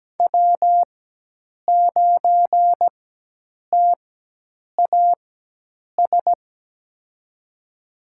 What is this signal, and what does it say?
Morse code, heard as a keyed tone, about 17 wpm, spaced as for 8 wpm, 700 Hz. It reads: W9TAS
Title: USA Call Signs - 5 Characters - 4x Character Spacing - 17wpm